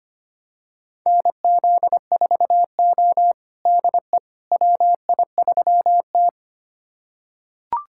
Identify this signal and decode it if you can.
Morse code, heard as a keyed tone, about 25 wpm, 700 Hz. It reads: NZ4O DE WI3T
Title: USA Callsign DE Callsign - 4 Characters - 25wpm